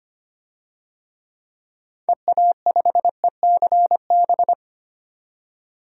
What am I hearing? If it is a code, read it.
Morse code, heard as a keyed tone, about 25 wpm, 700 Hz. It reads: EA5ECB
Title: DXCC-100 Call Signs - 6 Characters - 25wpm